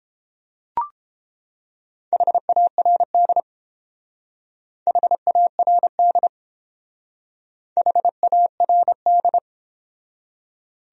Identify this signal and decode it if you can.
Morse code, heard as a keyed tone, about 29 wpm, 700 Hz. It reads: HARD HARD HARD